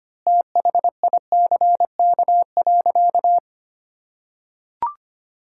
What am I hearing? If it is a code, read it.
Morse code, heard as a keyed tone, about 25 wpm, 700 Hz. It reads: THICK.
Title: Sentences from Top 1600 Words 25wpm